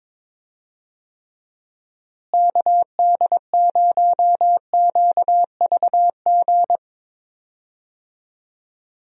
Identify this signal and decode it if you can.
Morse code, heard as a keyed tone, about 22 wpm, 700 Hz. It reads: KD0QVG
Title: USA Call Signs - 6 Characters - Repeated 3x - 22wpm